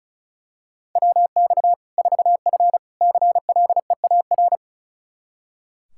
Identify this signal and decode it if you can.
Morse code, heard as a keyed tone, about 36 wpm, 700 Hz. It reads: WX 4F CLEAR
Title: Practice QSO - Exchange 2 - Info - 35wpm